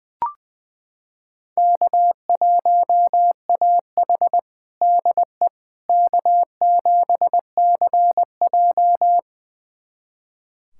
Morse code, heard as a keyed tone, about 21 wpm, 700 Hz. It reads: K1AH DE K7CJ
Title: USA Callsign DE Callsign - 4 Characters - 20wpm